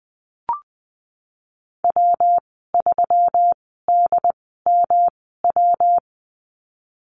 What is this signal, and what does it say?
Morse code, heard as a keyed tone, about 20 wpm, spaced as for 15 wpm, 700 Hz. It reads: W3DMW
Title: 219 Lesson 35g - Number 0 - Call Signs 20wpm